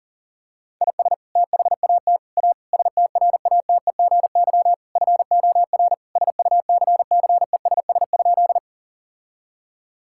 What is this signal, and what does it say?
Morse code, heard as a keyed tone, about 37 wpm, 700 Hz. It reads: IS THAT A STRATEGY FOR SUCCESS?
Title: Sentences from Top 1000 Words - Review Part 5 - 40wpm